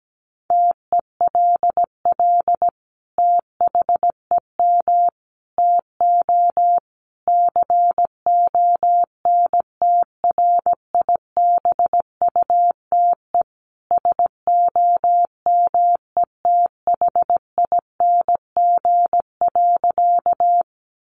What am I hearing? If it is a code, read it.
Morse code, heard as a keyed tone, about 17 wpm, 700 Hz. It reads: TELL THEM TO CONTRIBUTE SOMETHING.
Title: Sentences from Top 1400 Words 17wpm